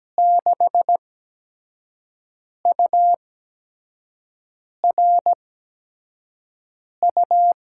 Morse code, heard as a keyed tone, about 17 wpm, spaced as for 5 wpm, 700 Hz. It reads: 6URU
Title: USA Call Signs - 6 Characters - 8x Character Spacing - 17wpm